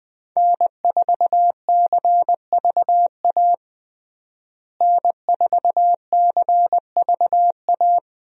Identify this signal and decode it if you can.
Morse code, heard as a keyed tone, about 20 wpm, 700 Hz. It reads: N4CVA N4CVA
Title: USA Call Signs - 5 Characters - Repeated 3x - 20wpm